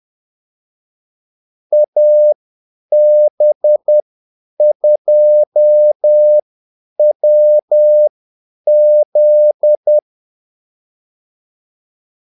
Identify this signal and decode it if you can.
Morse code, heard as a keyed tone, about 10 wpm, spaced as for 8 wpm, 600 Hz. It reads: AB2WZ
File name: Callsigns-10-8wpm